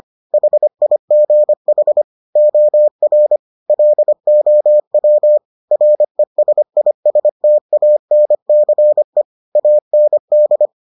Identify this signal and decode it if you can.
Morse code, heard as a keyed tone, about 25 wpm, 600 Hz. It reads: HIGH OR LOW RESISTANCE AND